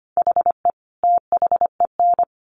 Morse code, heard as a keyed tone, about 25 wpm, 700 Hz. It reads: HE THEN
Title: Sentences from Top 300 Words 25wpm